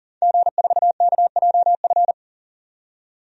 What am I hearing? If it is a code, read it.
Morse code, heard as a keyed tone, about 40 wpm, 700 Hz. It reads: G4KJF
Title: DXCC-100 Call Signs - 5 Characters - 40wpm